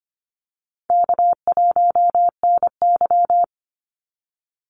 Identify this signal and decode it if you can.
Morse code, heard as a keyed tone, about 25 wpm, 700 Hz. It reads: K1NY
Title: Easy Call Signs - 25wpm